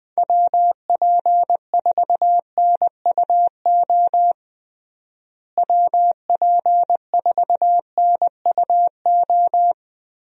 Morse code, heard as a keyed tone, about 20 wpm, 700 Hz. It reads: WP4NUO WP4NUO
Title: USA Call Signs - 6 Characters - Repeated 3x - 20wpm